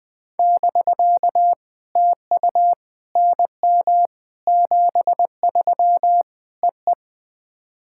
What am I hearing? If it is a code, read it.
Morse code, heard as a keyed tone, about 20 wpm, 700 Hz. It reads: <BK> TU NM 73 EE
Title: POTA QSO Practice - 20wpm